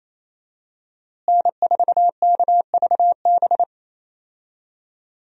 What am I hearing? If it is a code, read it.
Morse code, heard as a keyed tone, about 28 wpm, 700 Hz. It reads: N4KVB
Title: USA Call Signs - 5 Characters - Repeated 3x - 28wpm